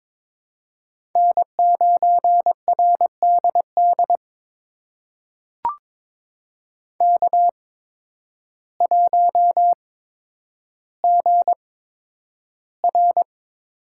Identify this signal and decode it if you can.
Morse code, heard as a keyed tone, about 21 wpm, spaced as for 10 wpm, 700 Hz. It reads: N9RDD K1GR
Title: USA Call Signs - 5 Characters - 8x Character Spacing - 22wpm